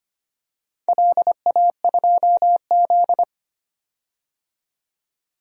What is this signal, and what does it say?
Morse code, heard as a keyed tone, about 25 wpm, 700 Hz. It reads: LA2Z
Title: DXCC-100 Call Signs - 4 Characters - 25wpm